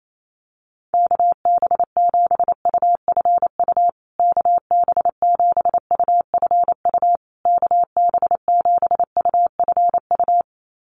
Morse code, heard as a keyed tone, about 28 wpm, 700 Hz. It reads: KB7UFU KB7UFU KB7UFU
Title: USA Call Signs Repeated 3x 28wpm